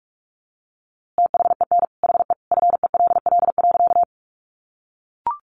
Morse code, heard as a keyed tone, about 44 wpm, 700 Hz. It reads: THEN HE FELL.
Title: A Sound Of Thunder 45wpm